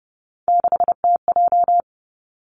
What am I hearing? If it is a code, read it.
Morse code, heard as a keyed tone, about 31 wpm, 700 Hz. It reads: 6TJ
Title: Three Challenging Letter-Number Combinations 30wpm